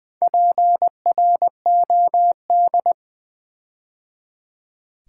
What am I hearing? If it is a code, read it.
Morse code, heard as a keyed tone, about 20 wpm, 700 Hz. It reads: PROD
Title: Build A Word - 4 Letters At A Time - 12-Letter Words - 20wpm